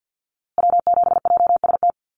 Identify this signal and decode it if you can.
Morse code, heard as a keyed tone, about 50 wpm, 700 Hz. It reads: W7JST
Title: USA Call Signs Repeated 2x 50wpm